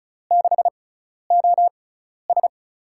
Morse code, heard as a keyed tone, about 36 wpm, spaced as for 12 wpm, 700 Hz. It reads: 6OS
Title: USA Call Signs - 5 Characters - 6x Character Spacing - 35wpm